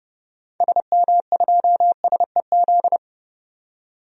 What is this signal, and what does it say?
Morse code, heard as a keyed tone, about 30 wpm, 700 Hz. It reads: SM2SEZ
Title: DXCC-100 Call Signs - 6 Characters - 30wpm